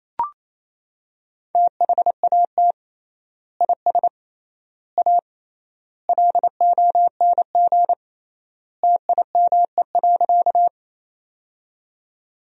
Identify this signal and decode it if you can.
Morse code, heard as a keyed tone, about 28 wpm, 700 Hz. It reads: THAT IS A LONG TIME.
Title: 3x Word-Spacing - Sentences from Top 100 Words - 28wpm